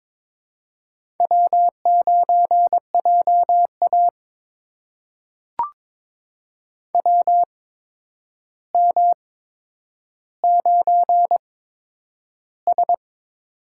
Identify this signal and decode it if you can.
Morse code, heard as a keyed tone, about 22 wpm, spaced as for 9 wpm, 700 Hz. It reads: W9JA WM9S
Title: USA Call Signs - 4 Characters - 8x Character Spacing - 22wpm